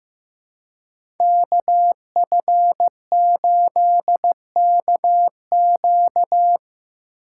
Morse code, heard as a keyed tone, about 16 wpm, 700 Hz. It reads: KF8KQ
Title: USA Call Signs - 4x Character Spacing - 15wpm